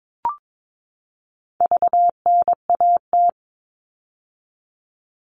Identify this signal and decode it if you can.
Morse code, heard as a keyed tone, about 22 wpm, 700 Hz. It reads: VNAT
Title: All Cut Numbers - 4-Digit 22wpm